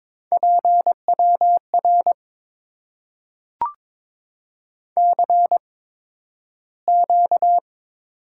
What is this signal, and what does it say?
Morse code, heard as a keyed tone, about 22 wpm, spaced as for 12 wpm, 700 Hz. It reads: PWR CQ
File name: Top 90 QSO Elements - 8x Character Spacing - 22wpm